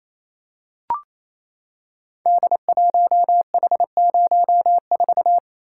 Morse code, heard as a keyed tone, about 28 wpm, 700 Hz. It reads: D1H04
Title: Code Groups - 5 Letter:Numbers - 28wpm